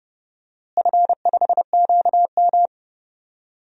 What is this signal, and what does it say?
Morse code, heard as a keyed tone, about 30 wpm, 700 Hz. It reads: F5QM
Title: DXCC-100 Call Signs - 4 Characters - Repeated 3x - 30wpm